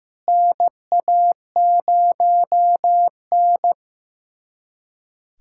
Morse code, heard as a keyed tone, about 15 wpm, 700 Hz. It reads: NA0N
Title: USA Callsign DE Callsign - 4 Characters - 15wpm